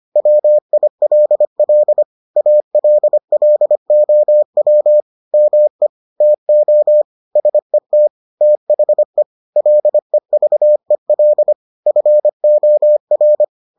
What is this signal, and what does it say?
Morse code, heard as a keyed tone, about 25 wpm, 600 Hz. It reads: WILL ALLOW ME TO SET THE LEVEL FOR